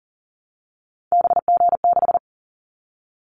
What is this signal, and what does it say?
Morse code, heard as a keyed tone, about 38 wpm, 700 Hz. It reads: BG6
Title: Callsign Prefixes for Top 100 DXCC - 40wpm